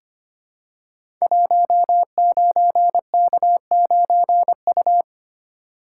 Morse code, heard as a keyed tone, about 25 wpm, 700 Hz. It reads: 19K9U
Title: Code Groups - 5 Letter:Numbers - 25wpm